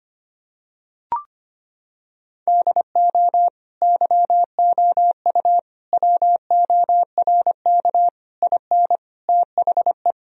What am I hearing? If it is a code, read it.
Morse code, heard as a keyed tone, about 25 wpm, 700 Hz. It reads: DO YOU WORK IN THE
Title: Sentences from Top 1000 Words - Review Part 4 - 25wpm